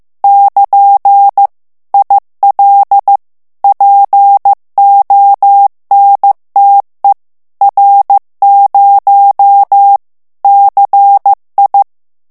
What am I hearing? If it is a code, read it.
Morse code, heard as a keyed tone, about 15 wpm, 800 Hz. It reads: <KN> IL PONTE R0 CI
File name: ITALIEN texte clair 1 vitesse 14 mots minute